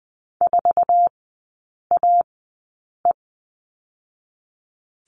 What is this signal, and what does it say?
Morse code, heard as a keyed tone, about 20 wpm, spaced as for 8 wpm, 700 Hz. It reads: 4AE
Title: 047 Lesson 9e - Number 4 - Three Characters 20wpm